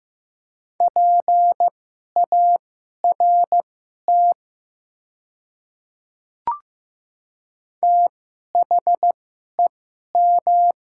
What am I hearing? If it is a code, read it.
Morse code, heard as a keyed tone, about 15 wpm, spaced as for 11 wpm, 700 Hz. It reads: PART THEM
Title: 2x Character Spacing - Top 200 Words - 15wpm